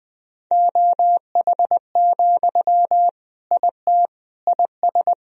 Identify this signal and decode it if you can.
Morse code, heard as a keyed tone, about 20 wpm, 700 Hz. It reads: OH, IT IS
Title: Sentences from Top 1000 Words - Review Part 3 - 20wpm